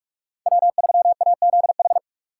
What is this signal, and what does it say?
Morse code, heard as a keyed tone, about 44 wpm, 700 Hz. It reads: W3AZH
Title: USA Call Signs - 5 Characters - 4x Character Spacing - 45wpm